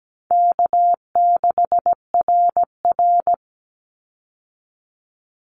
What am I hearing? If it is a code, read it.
Morse code, heard as a keyed tone, about 17 wpm, 700 Hz. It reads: K6RR
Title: Easy Call Signs Repeated 2x - 17wpm